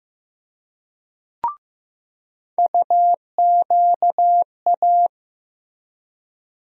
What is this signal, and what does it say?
Morse code, heard as a keyed tone, about 15 wpm, 700 Hz. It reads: UQA
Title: USA Call Sign Postfixes - 3 Letters 15wpm